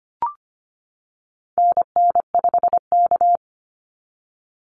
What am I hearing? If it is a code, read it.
Morse code, heard as a keyed tone, about 25 wpm, 700 Hz. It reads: NN5K
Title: NA SOTA Activator Callsigns 25wpm